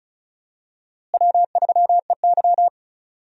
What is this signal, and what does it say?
Morse code, heard as a keyed tone, about 36 wpm, 700 Hz. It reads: W3EY
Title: USA Call Signs - 4 Characters - Repeated 2x - 35wpm